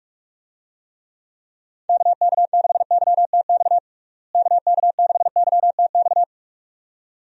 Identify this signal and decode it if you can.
Morse code, heard as a keyed tone, about 46 wpm, 700 Hz. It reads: KK6YTX KK6YTX
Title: USA Call Signs - 6 Characters - Repeated 2x - 45wpm